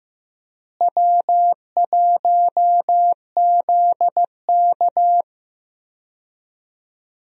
Morse code, heard as a keyed tone, about 14 wpm, 700 Hz. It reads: W1ZK
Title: USA Call Signs - 4 Characters - Repeated 3x - 15wpm